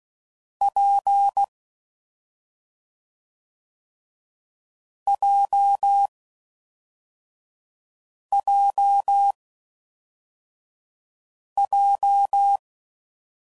Morse code, heard as a keyed tone, about 16 wpm, 800 Hz. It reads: P J J J
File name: Combinations / Difficult Combos / AWJ1